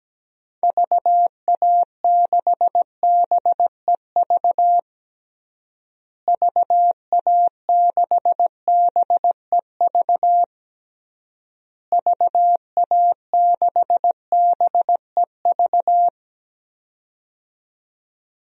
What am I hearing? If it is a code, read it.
Morse code, heard as a keyed tone, about 17 wpm, 700 Hz. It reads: VA6BEV VA6BEV VA6BEV
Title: DXCC-100 Call Signs - 6 Characters - Repeated 3x - 17wpm